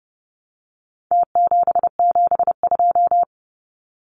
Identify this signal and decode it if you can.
Morse code, heard as a keyed tone, about 30 wpm, 700 Hz. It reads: T772
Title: Numbers - Contest Style - 4-Digit 30wpm